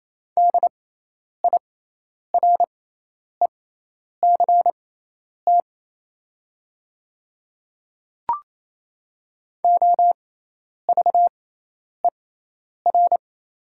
Morse code, heard as a keyed tone, about 27 wpm, spaced as for 10 wpm, 700 Hz. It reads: DIRECT OVER
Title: 6x Character Spacing - Top 500 Words - 28wpm